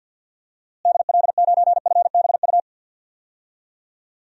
Morse code, heard as a keyed tone, about 48 wpm, 700 Hz. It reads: DL9FBU
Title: DXCC-100 Call Signs - 6 Characters - Repeated 3x - 50wpm